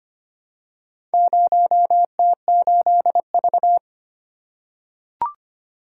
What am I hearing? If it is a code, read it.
Morse code, heard as a keyed tone, about 25 wpm, 700 Hz. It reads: 0T8V
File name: practice / 25wpm / Four Letter-Number Combinations - 25wpm